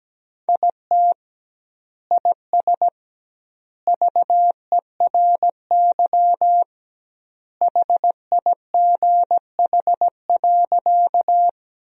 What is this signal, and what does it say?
Morse code, heard as a keyed tone, about 17 wpm, 700 Hz. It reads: IT IS VERY HIGH.
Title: 2x Word-Spacing - Sentences from Top 200 Words - 17wpm